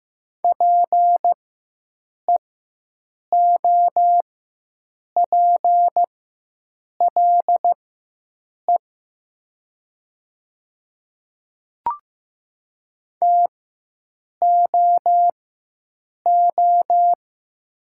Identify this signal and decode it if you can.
Morse code, heard as a keyed tone, about 15 wpm, spaced as for 7 wpm, 700 Hz. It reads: PEOPLE TOO
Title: 4x Character Spacing - Top 200 Words - 15wpm